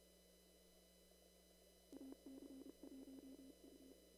Morse code, heard as a keyed tone, about 30 wpm, 270 Hz. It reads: AC1U
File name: CW / kt5j wpx cw-op-k5pi